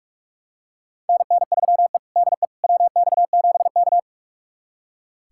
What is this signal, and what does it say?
Morse code, heard as a keyed tone, about 42 wpm, 700 Hz. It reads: NN3E DE WX7K
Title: USA Callsign DE Callsign - 4 Characters - 45wpm